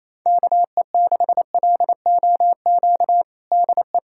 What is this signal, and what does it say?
Morse code, heard as a keyed tone, about 28 wpm, 700 Hz. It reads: KE6LOQ DE
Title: USA Callsign DE Callsign 28wpm